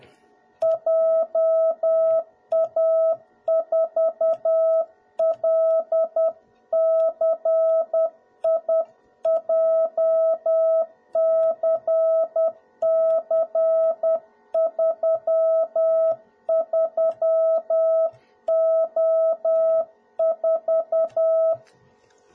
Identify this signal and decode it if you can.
Morse code, heard as a keyed tone, about 10 wpm, 650 Hz. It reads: JA4LCI JCC33O4
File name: jh8cbh / 2021cw / cw01b